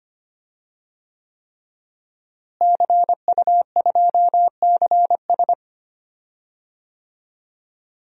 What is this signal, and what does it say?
Morse code, heard as a keyed tone, about 25 wpm, 700 Hz. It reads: CU2CS
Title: DXCC-100 Call Signs - 5 Characters - Repeated 3x - 25wpm